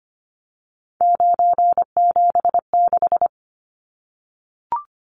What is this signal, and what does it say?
Morse code, heard as a keyed tone, about 25 wpm, 700 Hz. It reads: 976
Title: Numbers - 3-Digit 25wpm